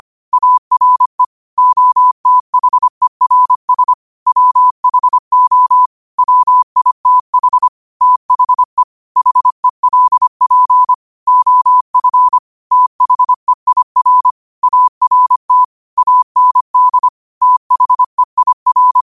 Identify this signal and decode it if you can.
Morse code, heard as a keyed tone, about 25 wpm, 1000 Hz. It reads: ARE OTHERS WHO WITH THE HELP OF THEIR ART AND THEIR
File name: qotdfiles / 25wpm